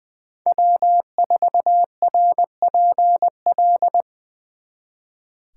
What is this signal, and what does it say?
Morse code, heard as a keyed tone, about 23 wpm, 700 Hz. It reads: W4RPL
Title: Speed Racing - USA 1x3 Call Signs 20wpm